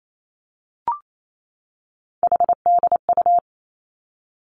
Morse code, heard as a keyed tone, about 28 wpm, 700 Hz. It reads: HDU
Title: Three Challenging Letter-Number Combinations 28wpm